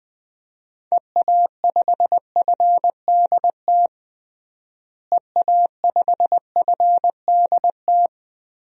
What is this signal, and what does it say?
Morse code, heard as a keyed tone, about 20 wpm, 700 Hz. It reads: EA5FDT EA5FDT
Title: DXCC-100 Call Signs - 6 Characters - Repeated 2x - 20wpm